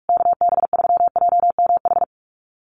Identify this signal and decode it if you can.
Morse code, heard as a keyed tone, about 44 wpm, 700 Hz. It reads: KB3JMH
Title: USA Call Signs Repeated 3x 45wpm